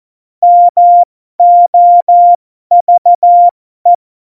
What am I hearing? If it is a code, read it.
Morse code, heard as a keyed tone, about 14 wpm, spaced as for 12 wpm, 700 Hz. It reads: MOVE
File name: Morsekurs / Witze